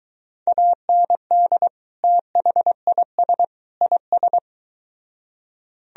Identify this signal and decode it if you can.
Morse code, heard as a keyed tone, about 23 wpm, 700 Hz. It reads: AND THIS IS